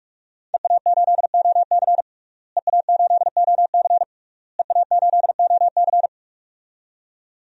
Morse code, heard as a keyed tone, about 46 wpm, 700 Hz. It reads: EA8OC EA8OC EA8OC
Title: DXCC-100 Call Signs - 5 Characters - Repeated 3x - 45wpm